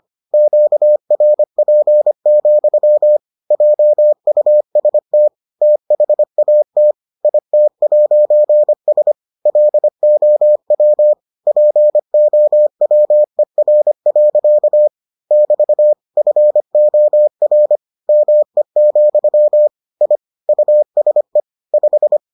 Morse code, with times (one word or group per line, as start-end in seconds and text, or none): 0.3-3.2 QRP,
3.5-5.3 JUST
5.6-6.9 THAT
7.2-9.1 IT'S
9.5-11.1 LOW
11.5-14.9 POWER.
15.3-15.9 <BT>
16.2-17.8 FOR
18.1-19.7 ME,
20.0-20.2 I
20.5-21.4 USE
21.7-22.2 5